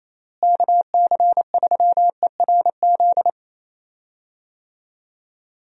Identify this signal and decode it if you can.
Morse code, heard as a keyed tone, about 28 wpm, 700 Hz. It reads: KC3ERZ
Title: USA Call Signs - 6 Characters - Repeated 3x - 28wpm